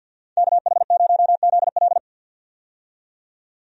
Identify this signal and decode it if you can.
Morse code, heard as a keyed tone, about 50 wpm, 700 Hz. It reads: KH0ZL
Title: USA Call Signs - 5 Characters - 50wpm